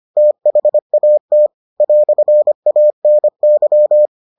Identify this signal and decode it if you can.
Morse code, heard as a keyed tone, about 25 wpm, 600 Hz. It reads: THAT "ANY